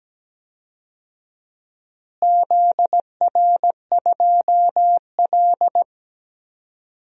Morse code, heard as a keyed tone, about 17 wpm, 700 Hz. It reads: ZR2L
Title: DXCC-100 Call Signs - 4 Characters - Repeated 2x - 17wpm